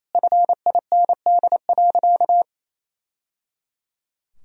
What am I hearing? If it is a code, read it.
Morse code, heard as a keyed tone, about 28 wpm, 700 Hz. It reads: FIND.
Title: Sentences from Top 1000 Words - Review Part 2 - 28wpm